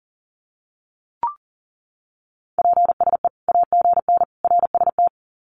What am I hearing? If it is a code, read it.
Morse code, heard as a keyed tone, about 40 wpm, 700 Hz. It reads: PSE AGN RST
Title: Common QSO Phrases 40wpm